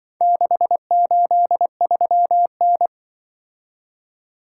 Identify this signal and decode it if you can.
Morse code, heard as a keyed tone, about 24 wpm, 700 Hz. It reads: 683N
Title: Speed-Racing - Four Letter-Number Combinations - 20wpm